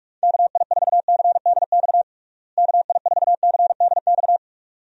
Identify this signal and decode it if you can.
Morse code, heard as a keyed tone, about 46 wpm, 700 Hz. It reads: KI4CDX KI4CDX
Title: USA Call Signs - 6 Characters - Repeated 2x - 45wpm